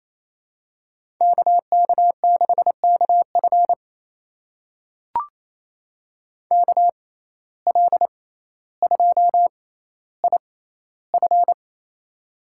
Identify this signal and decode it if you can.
Morse code, heard as a keyed tone, about 28 wpm, spaced as for 14 wpm, 700 Hz. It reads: KK6KF KL2IF
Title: USA Call Signs - 5 Characters - 6x Character Spacing - 28wpm